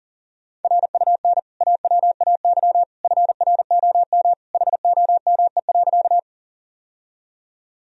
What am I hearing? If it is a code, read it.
Morse code, heard as a keyed tone, about 37 wpm, 700 Hz. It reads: RUN AWAY FROM HOME.
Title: Sentences from Top 1000 Words - Review Part 1 - 40wpm